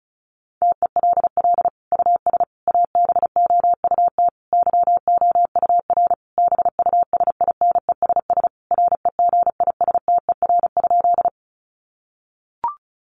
Morse code, heard as a keyed tone, about 34 wpm, 700 Hz. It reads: TELL US ABOUT YOUR BUSINESS REGISTER?
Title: Sentences from Top 1200 Words 35wpm